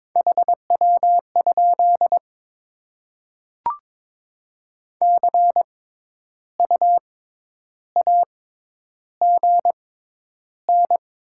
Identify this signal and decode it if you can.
Morse code, heard as a keyed tone, about 22 wpm, spaced as for 10 wpm, 700 Hz. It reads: HW? CU AGN
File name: Top 20 QSO Elements - 6x Character Spacing - 22wpm